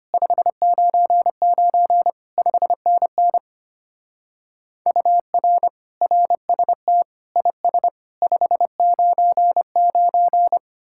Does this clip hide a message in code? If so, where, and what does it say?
Yes: Morse code from 0.1 to 10.6 s, heard as a keyed tone, about 32 wpm, 700 Hz. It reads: 599 5NN UR RST IS 599